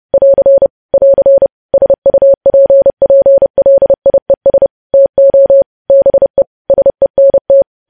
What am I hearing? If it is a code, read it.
Morse code, heard as a keyed tone, about 30 wpm, 550 Hz. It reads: <AR> <AR> SUPPLIES TO BE SENT
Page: news headlines in morse code at 30 wpm